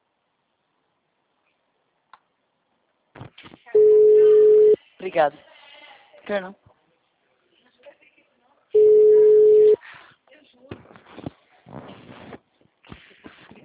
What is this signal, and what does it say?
Ringback tone (Europe, 425 Hz), with speech.
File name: out- -2004- - - .37661